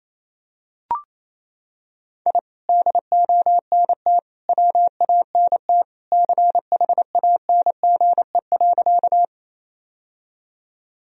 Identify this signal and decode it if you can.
Morse code, heard as a keyed tone, about 28 wpm, 700 Hz. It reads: I DONT WANT CHANGE.
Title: Sentences from Top 1000 Words - Review Part 1 - 28wpm